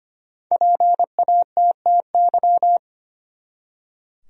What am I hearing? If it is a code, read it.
Morse code, heard as a keyed tone, about 25 wpm, 700 Hz. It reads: PATTY
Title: USA Common Names - Max 5 Letters - 25wpm